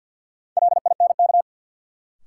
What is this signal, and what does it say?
Morse code, heard as a keyed tone, about 48 wpm, 700 Hz. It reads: LINK
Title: Sentences from Top 1500 Words - New Word Drill Alt - 50wpm